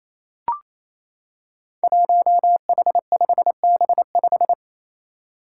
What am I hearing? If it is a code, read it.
Morse code, heard as a keyed tone, about 28 wpm, 700 Hz. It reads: 1H5B5
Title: Five Challenging Letter-Number Combinations 28wpm